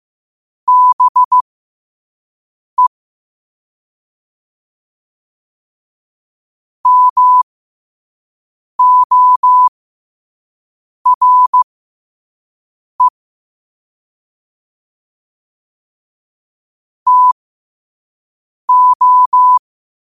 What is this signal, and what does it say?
Morse code, heard as a keyed tone, about 15 wpm, spaced as for 5 wpm, 1000 Hz. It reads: BE MORE TO
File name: qotdfiles / 5wpm